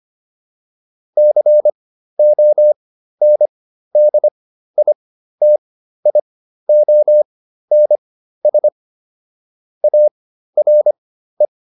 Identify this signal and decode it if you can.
Morse code, heard as a keyed tone, about 25 wpm, spaced as for 13 wpm, 600 Hz. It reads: CONDITIONS ARE
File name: QSO 407 13